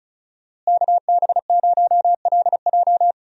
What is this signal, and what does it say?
Morse code, heard as a keyed tone, about 34 wpm, 700 Hz. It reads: KB0LJ
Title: USA Call Signs - 5 Characters - Repeated 2x - 35wpm